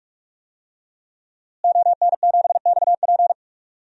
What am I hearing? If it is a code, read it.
Morse code, heard as a keyed tone, about 42 wpm, 700 Hz. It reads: ON7XP
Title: DXCC-100 Call Signs - 5 Characters - Repeated 2x - 45wpm